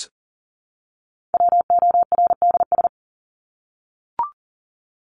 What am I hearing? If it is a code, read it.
Morse code, heard as a keyed tone, about 40 wpm, 700 Hz. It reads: WORDS
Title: ES:S Suffix - Top 100 Words with Base Words 40wpm